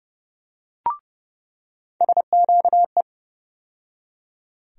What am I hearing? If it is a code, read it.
Morse code, heard as a keyed tone, about 32 wpm, 700 Hz. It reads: SQE
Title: USA Call Sign Postfixes - 3 Letters 30wpm